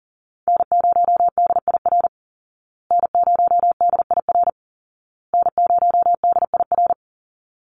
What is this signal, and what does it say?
Morse code, heard as a keyed tone, about 40 wpm, 700 Hz. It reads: N0DIR N0DIR N0DIR
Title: Call Signs Repeated 2x - 40wpm